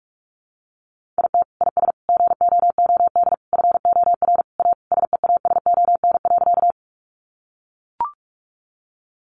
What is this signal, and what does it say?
Morse code, heard as a keyed tone, about 44 wpm, 700 Hz. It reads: IT IS GOOD FOR A SEASON.
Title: Sentences from Top 1400 Words 45wpm